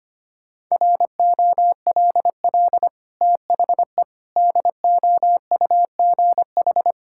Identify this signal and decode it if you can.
Morse code, heard as a keyed tone, about 25 wpm, 690 Hz. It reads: ROLL THE DOUGH
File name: Easy Sentences with Hints - 25wpm